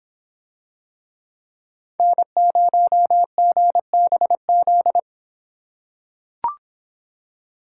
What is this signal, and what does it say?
Morse code, heard as a keyed tone, about 26 wpm, 700 Hz. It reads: N0GBZ